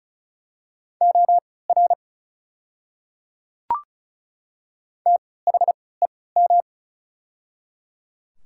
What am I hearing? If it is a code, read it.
Morse code, heard as a keyed tone, about 36 wpm, spaced as for 20 wpm, 700 Hz. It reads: OR THEM
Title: 3x Character Spacing - Top 200 Words - 35wpm